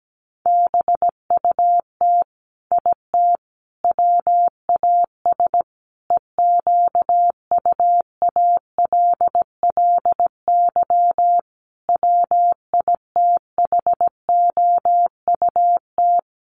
Morse code, heard as a keyed tone, about 17 wpm, 700 Hz. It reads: BUT IT WAS EQUALLY WITHOUT
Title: Sentences from Top 1400 Words 17wpm